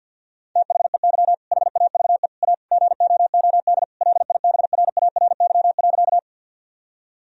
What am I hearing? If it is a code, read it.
Morse code, heard as a keyed tone, about 50 wpm, 700 Hz. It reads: THEY HAVE A GOOD LIBRARY.
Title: Sentences from Top 1000 Words - Review Part 4 - 50wpm